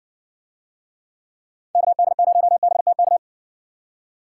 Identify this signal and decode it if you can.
Morse code, heard as a keyed tone, about 65 wpm, 700 Hz. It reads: KD0BTK